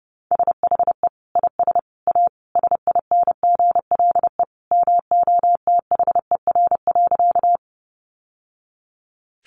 Morse code, heard as a keyed tone, about 30 wpm, 700 Hz. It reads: SHE IS A SINGLE MOTHER.
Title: Sentences from Top 700 Words 30wpm